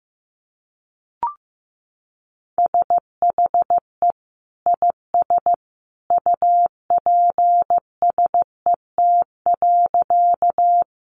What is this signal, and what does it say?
Morse code, heard as a keyed tone, about 15 wpm, 700 Hz. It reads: SHE IS UPSET.
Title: Sentences from Top 800 Words 15wpm